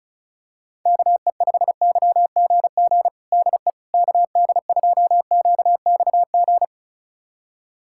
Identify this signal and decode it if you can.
Morse code, heard as a keyed tone, about 34 wpm, 700 Hz. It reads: KE5YGG DE KD2QXG
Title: USA Callsign DE Callsign - 6 Characters - 35wpm